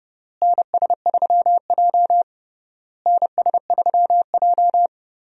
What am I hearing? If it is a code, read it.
Morse code, heard as a keyed tone, about 30 wpm, 700 Hz. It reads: NS3J NS3J
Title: USA Call Signs - 4 Characters - Repeated 3x - 30wpm